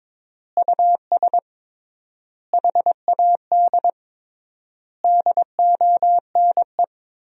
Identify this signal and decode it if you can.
Morse code, heard as a keyed tone, about 22 wpm, 700 Hz. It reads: US HAD DONE
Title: 3x Word-Spacing - Sentences from Top 500 Words - 22wpm